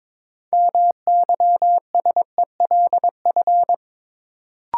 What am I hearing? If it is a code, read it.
Morse code, heard as a keyed tone, about 23 wpm, 700 Hz. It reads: MYSELF
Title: Sentences from Top 1300 Words - New Word Drill - 22wpm